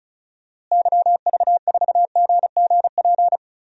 Morse code, heard as a keyed tone, about 34 wpm, 700 Hz. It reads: YV4GGP
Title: DXCC-100 Call Signs - 6 Characters - Repeated 2x - 35wpm